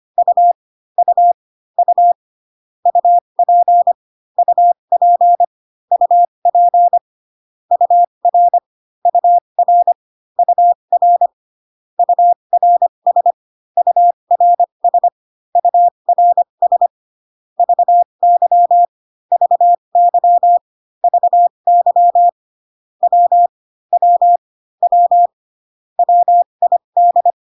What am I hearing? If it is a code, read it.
Morse code, heard as a keyed tone, about 25 wpm, spaced as for 22 wpm, 700 Hz. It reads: U U U UP UP UP UR UR UR URS URS URS VY VY VY W W W WID
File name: CW common abbreviations alphabetical repeated 3 times 25 18wpm